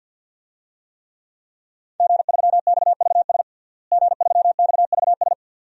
Morse code, heard as a keyed tone, about 53 wpm, 700 Hz. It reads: G3XVS G3XVS
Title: DXCC-100 Call Signs - 5 Characters - Repeated 2x - 50wpm